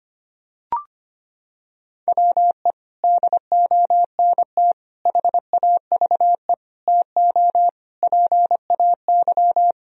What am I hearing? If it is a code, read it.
Morse code, heard as a keyed tone, about 25 wpm, 700 Hz. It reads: WE DONT HAVE TO PAY
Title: Sentences from Top 1000 Words - Review Part 2 - 25wpm